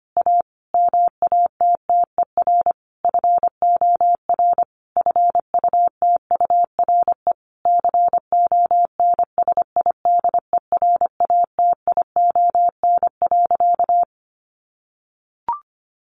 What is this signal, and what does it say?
Morse code, heard as a keyed tone, about 25 wpm, 700 Hz. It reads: A MATTER FOR FUTURE CONSIDERATION.
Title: Sentences from Top 1500 Words 25wpm